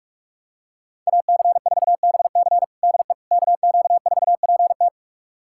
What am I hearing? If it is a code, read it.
Morse code, heard as a keyed tone, about 42 wpm, 700 Hz. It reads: AC4BC DE KQ4PT
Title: USA Callsign DE Callsign - 5 Characters - 45wpm